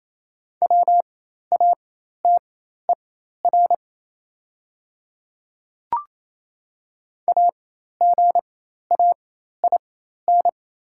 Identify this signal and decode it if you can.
Morse code, heard as a keyed tone, about 28 wpm, spaced as for 13 wpm, 700 Hz. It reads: WATER AGAIN
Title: 4x Character Spacing - Top 300 Words - 28wpm